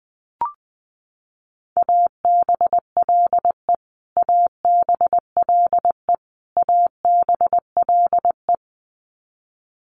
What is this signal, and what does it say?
Morse code, heard as a keyed tone, about 19 wpm, 700 Hz. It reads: ABLE ABLE ABLE
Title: Top 500 Words Repeated 20wpm